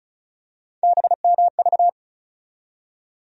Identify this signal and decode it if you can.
Morse code, heard as a keyed tone, about 34 wpm, 700 Hz. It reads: BMV
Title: USA Call Sign Postfixes - 3 Letters 35wpm